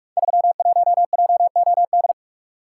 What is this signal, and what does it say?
Morse code, heard as a keyed tone, about 44 wpm, 700 Hz. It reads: 31JOD
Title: Code Groups - 5 Letter:Numbers - 45wpm